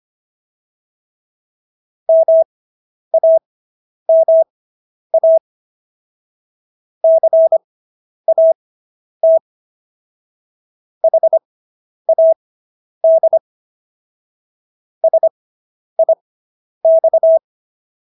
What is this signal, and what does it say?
Morse code, heard as a keyed tone, about 25 wpm, spaced as for 10 wpm, 650 Hz. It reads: MAMA CAT HAD SIX